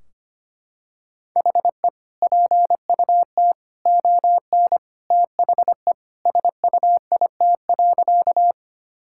Morse code, heard as a keyed tone, about 25 wpm, 700 Hz. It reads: HE PUT ON THE SUIT.
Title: Sentences from Top 1600 Words 25wpm